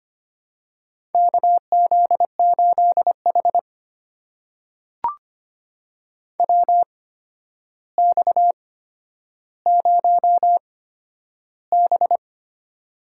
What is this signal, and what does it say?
Morse code, heard as a keyed tone, about 25 wpm, spaced as for 11 wpm, 700 Hz. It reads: KZ8H WX0B
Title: USA Call Signs - 8x Character Spacing - 25wpm